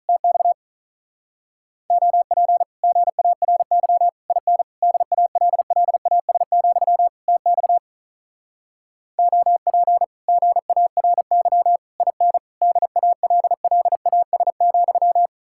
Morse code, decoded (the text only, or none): TX OP GARY IN DALLAS, TX OP GARY IN DALLAS,